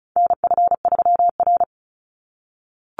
Morse code, heard as a keyed tone, about 34 wpm, 700 Hz. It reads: NF3R
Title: Call Signs - 35wpm